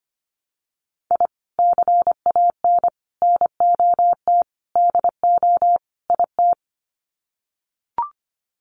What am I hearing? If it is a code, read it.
Morse code, heard as a keyed tone, about 25 wpm, 700 Hz. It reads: I CAN NOT DO IT
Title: Sets of 5 Words Encore 25wpm